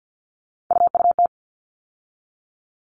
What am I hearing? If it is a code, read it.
Morse code, heard as a keyed tone, about 48 wpm, 700 Hz. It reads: UUT
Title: All Cut Numbers - 3-Digit 50wpm